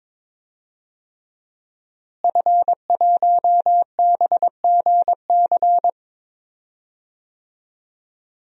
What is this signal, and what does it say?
Morse code, heard as a keyed tone, about 21 wpm, 700 Hz. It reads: F1BGC
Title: DXCC-100 Call Signs - 5 Characters - Repeated 3x - 22wpm